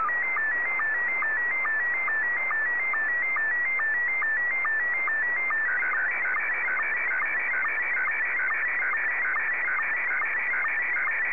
SSTV (slow-scan television), Scottie 1.
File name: sstv / S1-VK3DEK